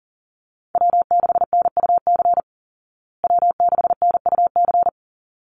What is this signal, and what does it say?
Morse code, heard as a keyed tone, about 40 wpm, 700 Hz. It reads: W6NUC W6NUC
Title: Call Signs Repeated 3x - 40wpm